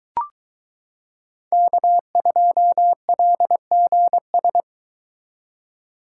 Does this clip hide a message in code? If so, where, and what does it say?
Yes: Morse code from 1.5 to 4.6 s, heard as a keyed tone, about 22 wpm, 700 Hz. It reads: K2LGS